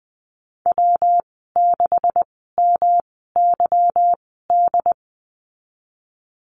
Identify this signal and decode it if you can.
Morse code, heard as a keyed tone, about 20 wpm, spaced as for 15 wpm, 700 Hz. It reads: W6MYD
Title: 179 Lesson 29g - Slash - Call Signs 20wpm